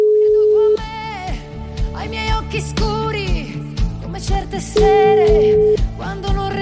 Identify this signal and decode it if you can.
Ringback tone (Europe, 425 Hz), with speech.